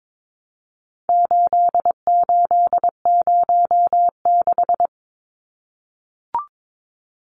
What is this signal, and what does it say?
Morse code, heard as a keyed tone, about 22 wpm, 700 Hz. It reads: 8806
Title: Numbers - 4-Digit 22wpm